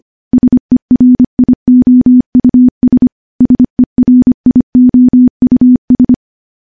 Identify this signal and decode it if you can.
Morse code, heard as a keyed tone, about 25 wpm, 250 Hz. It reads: SERIOUS SERIOUS
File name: morse-training-25WPM